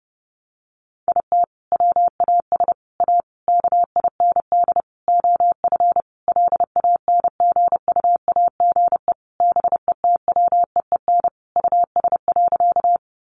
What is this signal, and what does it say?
Morse code, heard as a keyed tone, about 30 wpm, 700 Hz. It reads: IT WAS A KIND OF LANGUAGE BETWEEN US.
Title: Sentences from Top 500 Words 30wpm